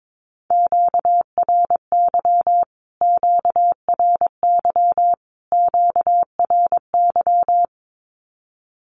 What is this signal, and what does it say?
Morse code, heard as a keyed tone, about 22 wpm, 700 Hz. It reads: QRY QRY QRY
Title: Q-Codes Repeated 3x 22wpm